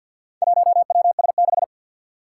Morse code, heard as a keyed tone, about 53 wpm, 700 Hz. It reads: 1WS6
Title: Four Challenging Letter-Number Combinations 50wpm